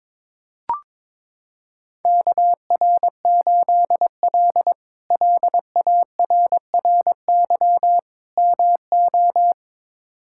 Morse code, heard as a keyed tone, about 22 wpm, 700 Hz. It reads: KR8L LARRY MO
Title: K1USN SST QSO Practice - 22wpm